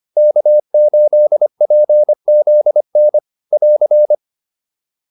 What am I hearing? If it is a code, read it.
Morse code, heard as a keyed tone, about 25 wpm, 600 Hz. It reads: K8PZN <AR>